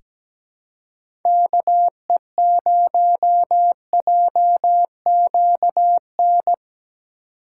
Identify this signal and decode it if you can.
Morse code, heard as a keyed tone, about 17 wpm, 700 Hz. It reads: KE0JQN
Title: USA Call Signs - 6 Characters - 2x Character Spacing - 17wpm